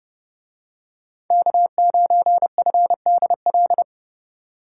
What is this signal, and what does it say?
Morse code, heard as a keyed tone, about 29 wpm, 700 Hz. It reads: K9FDL